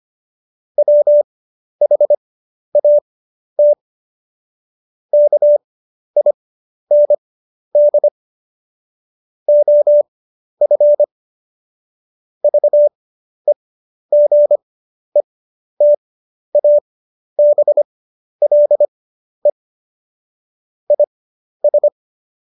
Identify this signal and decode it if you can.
Morse code, heard as a keyed tone, about 25 wpm, spaced as for 11 wpm, 590 Hz. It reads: WHAT KIND OF VEGETABLE IS
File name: Jokes-25-char-10-eff